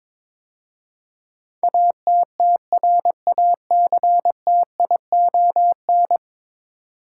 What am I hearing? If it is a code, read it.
Morse code, heard as a keyed tone, about 21 wpm, 700 Hz. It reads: ATTRACTION
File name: practice / 22wpm / 10 Letter Words - First 3 Letters Given - 22wpm